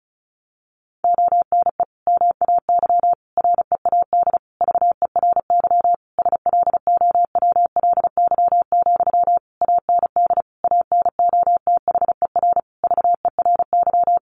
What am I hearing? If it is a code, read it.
Morse code, heard as a keyed tone, about 34 wpm, 700 Hz. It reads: ONE MAY READ VERY SLOWLY, AND ANOTHER VERY
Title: Sentences from Top 1100 Words 35wpm